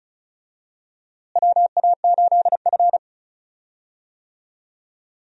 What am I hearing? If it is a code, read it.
Morse code, heard as a keyed tone, about 34 wpm, 700 Hz. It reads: WA8F
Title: USA Call Signs - 4 Characters - Repeated 3x - 35wpm